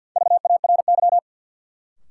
Morse code, heard as a keyed tone, about 53 wpm, 700 Hz. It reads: VARY
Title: Sentences from Top 1200 Words - New Word Drill - 50wpm